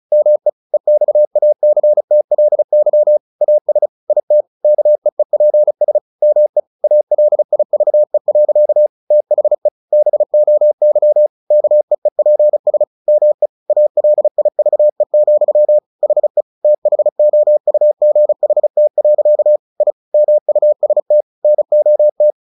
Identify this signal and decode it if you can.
Morse code, heard as a keyed tone, about 35 wpm, 600 Hz. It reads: ME EXACTLY AS IT KEEPS ME ALIVE. THE BOY KEEPS ME ALIVE, HE THOUGHT. I MUST NOT
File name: the old man and the sea 2 35wpm